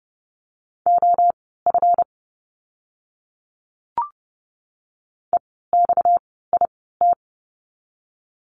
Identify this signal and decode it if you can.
Morse code, heard as a keyed tone, about 30 wpm, spaced as for 17 wpm, 700 Hz. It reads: OF EXIT
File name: course / 30:10wpm / 230 Lesson 37f - Letter x - Words 30wpm